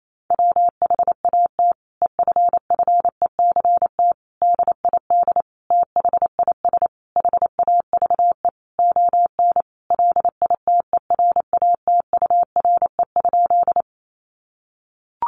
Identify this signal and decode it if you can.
Morse code, heard as a keyed tone, about 28 wpm, 700 Hz. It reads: WHAT EFFECT DID THIS HAVE ON LITERATURE?
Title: Sentences from Top 1000 Words 28wpm